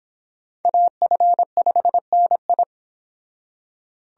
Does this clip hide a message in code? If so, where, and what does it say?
Yes: Morse code from 0.7 to 2.6 s, heard as a keyed tone, about 25 wpm, 700 Hz. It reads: AF5NI